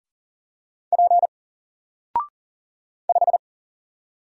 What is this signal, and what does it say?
Morse code, heard as a keyed tone, about 40 wpm, 700 Hz. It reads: P 5
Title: Single Letter-Number Alt 40wpm